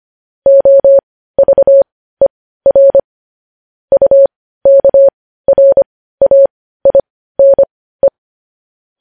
Morse code, heard as a keyed tone, about 25 wpm, spaced as for 15 wpm, 550 Hz. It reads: OVER UKRAINE
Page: news-headlines-in-morse-code-at-15-wpm